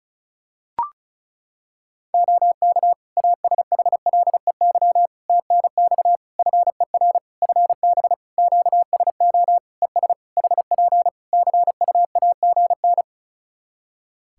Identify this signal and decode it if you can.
Morse code, heard as a keyed tone, about 34 wpm, 700 Hz. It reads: OK ASHLEY TNX FER FB QSO ES HP CUAGN
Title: Practice QSO - Exchange 4 - Ending - 35wpm